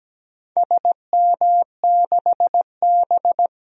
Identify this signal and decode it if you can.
Morse code, heard as a keyed tone, about 17 wpm, 700 Hz. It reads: SM6B
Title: DXCC-100 Call Signs - 4 Characters - 17wpm